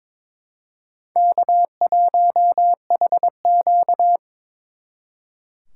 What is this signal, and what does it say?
Morse code, heard as a keyed tone, about 26 wpm, 700 Hz. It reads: K1HQ
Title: Speed Racing - USA 1x2 Call Signs 22wpm